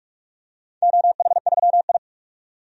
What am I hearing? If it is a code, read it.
Morse code, heard as a keyed tone, about 46 wpm, 700 Hz. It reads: OH3I
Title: DXCC-100 Call Signs - 4 Characters - 45wpm